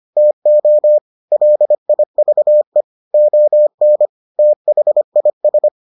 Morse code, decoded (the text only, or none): TO LIVE ON THIS